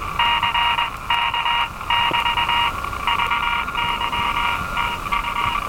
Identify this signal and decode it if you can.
Morse code, heard as a keyed tone, about 21 wpm, 950 Hz, with noise. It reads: CK <BT> VY TU
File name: attachments / RV3GM EW1CY